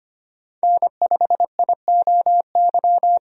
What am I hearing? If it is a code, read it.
Morse code, heard as a keyed tone, about 25 wpm, 700 Hz. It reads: N5IOY
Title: USA Call Signs - 5 Characters - 25wpm